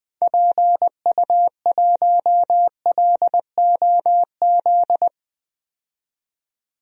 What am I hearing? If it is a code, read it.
Morse code, heard as a keyed tone, about 20 wpm, 700 Hz. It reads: PU1LOZ
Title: DXCC-100 Call Signs - 6 Characters - 20wpm